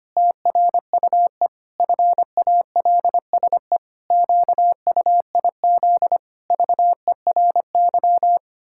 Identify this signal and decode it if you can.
Morse code, heard as a keyed tone, about 25 wpm, 700 Hz. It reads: TRUE FALSE QUIZ VERY
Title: Pangram Challenge v2 - 25wpm